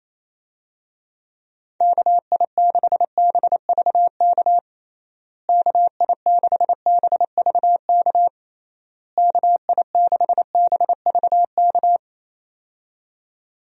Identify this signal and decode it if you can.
Morse code, heard as a keyed tone, about 28 wpm, 700 Hz. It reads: KI6BVK KI6BVK KI6BVK
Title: USA Call Signs - 6 Characters - Repeated 3x - 28wpm